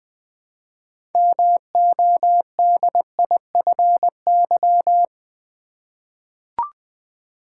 Morse code, heard as a keyed tone, about 20 wpm, 700 Hz. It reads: MODIFY
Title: 6 Letter Words - First 3 Letters Given - 20wpm